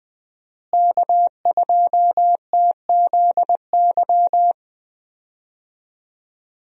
Morse code, heard as a keyed tone, about 20 wpm, 700 Hz. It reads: K2TZY
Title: USA Call Signs - 5 Characters - Repeated 3x - 20wpm